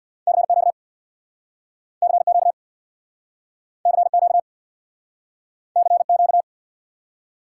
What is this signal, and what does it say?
Morse code, heard as a keyed tone, about 69 wpm, 700 Hz. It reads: CQ CQ CQ CQ